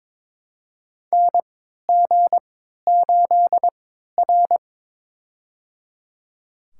Morse code, heard as a keyed tone, about 22 wpm, spaced as for 13 wpm, 700 Hz. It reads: NG8R
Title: USA Call Signs - 4 Characters - 3x Character Spacing - 22wpm